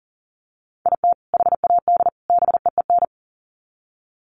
Morse code, heard as a keyed tone, about 40 wpm, 700 Hz. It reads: IT HAD BEEN
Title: Sets of 3 Words 40wpm